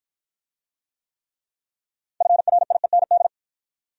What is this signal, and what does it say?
Morse code, heard as a keyed tone, about 53 wpm, 700 Hz. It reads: FRIEND